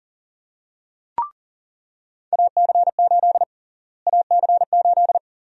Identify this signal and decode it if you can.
Morse code, heard as a keyed tone, about 39 wpm, 700 Hz. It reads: AC8 AC8
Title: USA Call Sign Prefixes - Repeated 2x - 40wpm